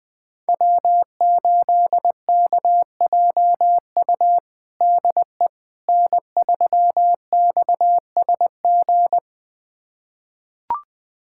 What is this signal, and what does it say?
Morse code, heard as a keyed tone, about 20 wpm, 700 Hz. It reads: W8KJU DE N3XSG
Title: USA Callsign DE Callsign - 5 Characters - 20wpm